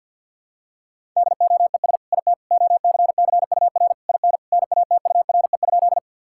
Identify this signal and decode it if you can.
Morse code, heard as a keyed tone, about 50 wpm, 700 Hz. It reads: DOES IT OCCUR IN NATURE?
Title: Sentences from Top 1000 Words - Review Part 6 - 50wpm